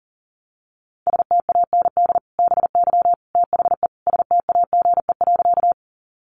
Morse code, heard as a keyed tone, about 40 wpm, 700 Hz. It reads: STAND BY THE STAGE.
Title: Sentences from Top 1200 Words 40wpm